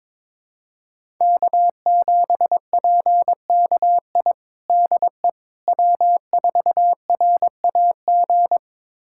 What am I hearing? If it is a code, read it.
Morse code, heard as a keyed tone, about 22 wpm, 700 Hz. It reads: K7PKI DE W4RAG
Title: USA Callsign DE Callsign - 5 Characters - 22wpm